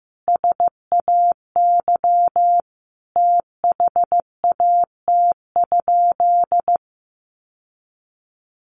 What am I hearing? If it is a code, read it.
Morse code, heard as a keyed tone, about 15 wpm, 700 Hz. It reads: SAY THAT?
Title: Sentences from Top 1500 Words 15wpm